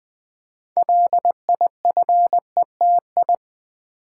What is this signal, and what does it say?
Morse code, heard as a keyed tone, about 20 wpm, 700 Hz. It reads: LIFETI
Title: Build A Word - 2 Letters At A Time - 8-Letter Words - 20wpm